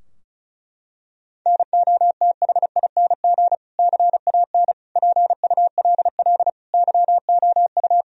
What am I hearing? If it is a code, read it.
Morse code, heard as a keyed tone, about 36 wpm, 700 Hz. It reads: NOTHING CAN PULL YOU
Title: Sentences from Top 1000 Words - Review Part 2 - 35wpm